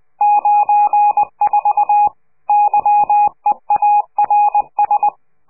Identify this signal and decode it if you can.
Morse code, heard as a keyed tone, about 20 wpm, 830 Hz. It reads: 94 YEARS
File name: CQ de PCH laatste bericht farewell